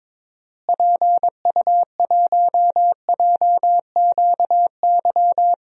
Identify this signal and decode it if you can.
Morse code, heard as a keyed tone, about 22 wpm, 700 Hz. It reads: PU1JQY
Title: DXCC-100 Call Signs - 6 Characters - Repeated 3x - 22wpm